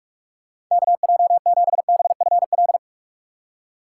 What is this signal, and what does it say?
Morse code, heard as a keyed tone, about 44 wpm, 700 Hz. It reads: KJ7BFL
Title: USA Call Signs - 6 Characters - 4x Character Spacing - 45wpm